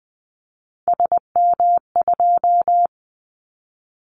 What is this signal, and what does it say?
Morse code, heard as a keyed tone, about 20 wpm, 700 Hz. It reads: SM2
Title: Callsign Prefixes for Top 25 DXCC - 20wpm